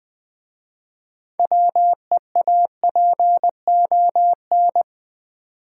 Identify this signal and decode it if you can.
Morse code, heard as a keyed tone, about 20 wpm, 700 Hz. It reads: WEAPON
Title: 6 Letter Words - First 3 Letters Given - 20wpm